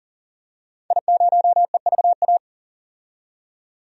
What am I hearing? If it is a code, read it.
Morse code, heard as a keyed tone, about 40 wpm, 700 Hz. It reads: I0EVA
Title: DXCC-100 Call Signs - 5 Characters - 40wpm